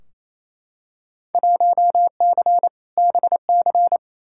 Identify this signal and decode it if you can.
Morse code, heard as a keyed tone, about 28 wpm, 700 Hz. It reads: 1C BC
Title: Field Day Exchanges Encore - 28wpm